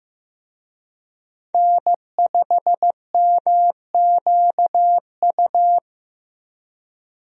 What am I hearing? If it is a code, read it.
Morse code, heard as a keyed tone, about 15 wpm, 700 Hz. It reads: N5MQU
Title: USA Call Signs - 5 Characters - Repeated 2x - 15wpm